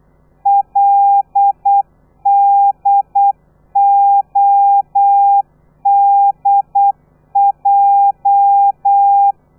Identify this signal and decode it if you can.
Morse code, heard as a keyed tone, about 8 wpm, 800 Hz. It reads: LDODJ
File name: telegrafia / i2tfj / Lez13 35